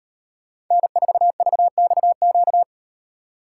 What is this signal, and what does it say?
Morse code, heard as a keyed tone, about 38 wpm, 700 Hz. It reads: N4VXQ